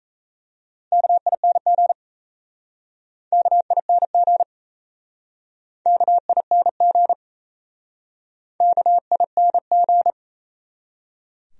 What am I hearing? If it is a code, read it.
Morse code, heard as a keyed tone, about 37 wpm, 700 Hz. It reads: KING KING KING KING